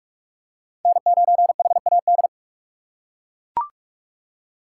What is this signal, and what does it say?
Morse code, heard as a keyed tone, about 40 wpm, 700 Hz. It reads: N9HAD